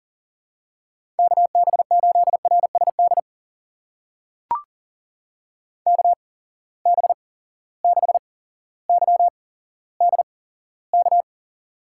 Morse code, heard as a keyed tone, about 40 wpm, spaced as for 17 wpm, 700 Hz. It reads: KB8RSD KB6YDK
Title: USA Call Signs - 6 Characters - 8x Character Spacing - 40wpm